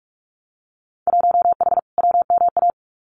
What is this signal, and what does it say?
Morse code, heard as a keyed tone, about 46 wpm, 700 Hz. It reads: 1H WMA
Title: Winter Field Day Exchanges 45wpm